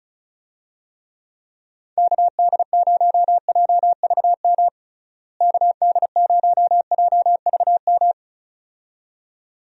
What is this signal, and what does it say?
Morse code, heard as a keyed tone, about 36 wpm, 700 Hz. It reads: KD0JVM KD0JVM
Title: USA Call Signs - 6 Characters - Repeated 2x - 35wpm